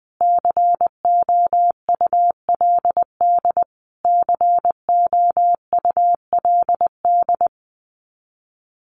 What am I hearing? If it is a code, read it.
Morse code, heard as a keyed tone, about 20 wpm, 700 Hz. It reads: COULD COULD
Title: Top 200 Words Repeated 20wpm